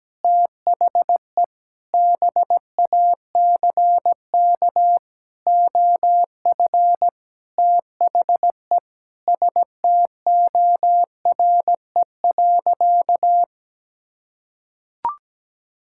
Morse code, heard as a keyed tone, about 17 wpm, 700 Hz. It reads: THE BACK OF THE STORE.
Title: Sentences from Top 1000 Words - Review Part 5 - 17wpm